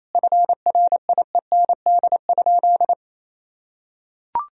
Morse code, heard as a keyed tone, about 28 wpm, 700 Hz. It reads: FRIEND?
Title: Sentences from Top 300 Words and Abbreviations 28wpm